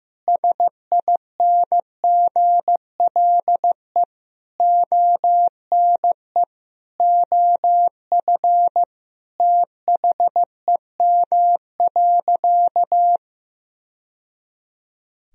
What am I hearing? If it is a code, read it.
Morse code, heard as a keyed tone, about 15 wpm, 700 Hz. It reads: SINGLE ONE OF THEM.
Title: Sentences from Top 1000 Words - Review Part 4 - 15wpm